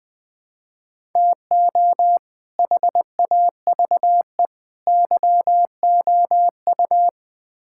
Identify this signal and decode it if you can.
Morse code, heard as a keyed tone, about 20 wpm, 700 Hz. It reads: TO HAVE YOU
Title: Build A Phrase - 5-Word Phrases - 20wpm